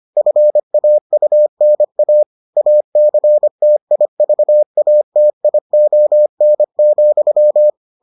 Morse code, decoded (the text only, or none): FAUNA ACTIVATION,